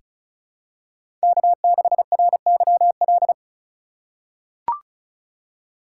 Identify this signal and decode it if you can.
Morse code, heard as a keyed tone, about 34 wpm, 700 Hz. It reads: K6RYL
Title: USA Call Signs - 5 Characters - 4x Character Spacing - 35wpm